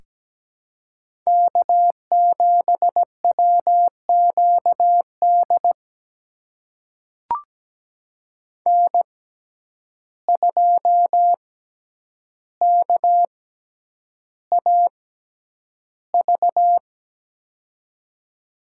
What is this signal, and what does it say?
Morse code, heard as a keyed tone, about 17 wpm, spaced as for 9 wpm, 700 Hz. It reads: K7WQD N2KAV
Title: USA Call Signs - 5 Characters - 6x Character Spacing - 17wpm